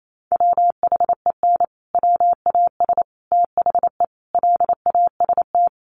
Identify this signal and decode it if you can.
Morse code, heard as a keyed tone, about 28 wpm, 700 Hz. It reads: WHEN WAS THE LAST
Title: Sets of 5 Words Encore 28wpm